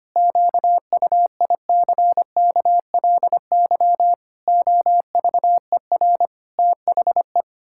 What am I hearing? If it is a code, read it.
Morse code, heard as a keyed tone, about 25 wpm, 700 Hz. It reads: QUICKLY OVER THE
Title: Pangram Challenge v2 - 25wpm